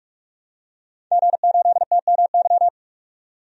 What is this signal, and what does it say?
Morse code, heard as a keyed tone, about 46 wpm, 700 Hz. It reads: G8TMY
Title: DXCC-100 Call Signs - 5 Characters - 45wpm